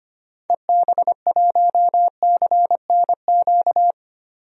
Morse code, heard as a keyed tone, about 25 wpm, 700 Hz. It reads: EB1CNQ
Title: DXCC-100 Call Signs - 6 Characters - 25wpm